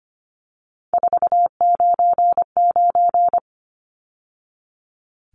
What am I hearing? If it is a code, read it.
Morse code, heard as a keyed tone, about 25 wpm, 700 Hz. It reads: 499
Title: Numbers - 3-Digit 25wpm